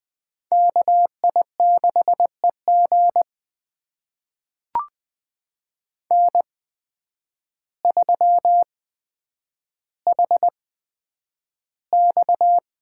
Morse code, heard as a keyed tone, about 20 wpm, spaced as for 9 wpm, 700 Hz. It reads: KI6EG N3HX
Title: USA Call Signs - 5 Characters - 8x Character Spacing - 20wpm